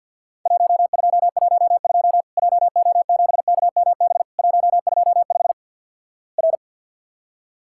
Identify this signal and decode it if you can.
Morse code, heard as a keyed tone, about 48 wpm, 700 Hz. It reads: 1212 JO7KMB 125 R
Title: Worked All Europe DX Contest - QTC Exchange with Repeats - 50wpm